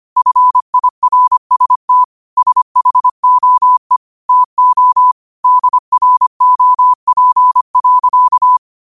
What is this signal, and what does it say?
Morse code, heard as a keyed tone, about 25 wpm, 1000 Hz. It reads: FIRST SHOE TO DROP.